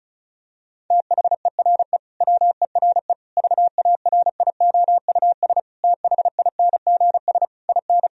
Morse code, heard as a keyed tone, about 34 wpm, 700 Hz. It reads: THERE WERE VARIOUS THINGS IN
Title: Sentences from Top 1000 Words - Review Part 4 - 35wpm